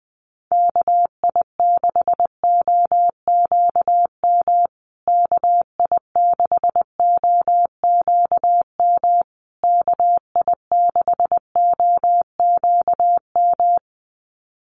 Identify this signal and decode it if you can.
Morse code, heard as a keyed tone, about 20 wpm, 700 Hz. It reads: KI6OQM KI6OQM KI6OQM
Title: USA Call Signs Repeated 3x 20wpm